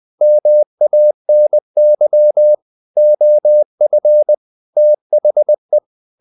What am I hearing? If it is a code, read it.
Morse code, heard as a keyed tone, about 20 wpm, 600 Hz. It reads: MANY OF THE
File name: the old man and the sea part 2 20wpm